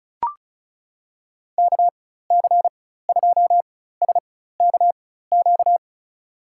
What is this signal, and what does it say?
Morse code, heard as a keyed tone, about 34 wpm, spaced as for 16 wpm, 690 Hz. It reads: KC2SKQ
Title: USA Call Signs - 4x Character Spacing - 35wpm